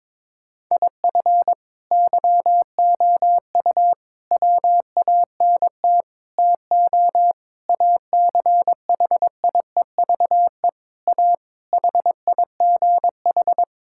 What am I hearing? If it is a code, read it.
Morse code, heard as a keyed tone, about 21 wpm, 700 Hz. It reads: IF YOU WANT TO ACHIEVE A HIGH
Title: Sentences from Top 1000 Words - Review Part 6 - 22wpm